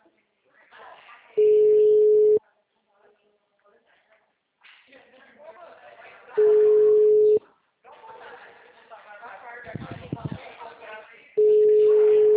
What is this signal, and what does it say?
Ringback tone (Europe, 425 Hz), with speech.
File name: out- -2009- - - .59715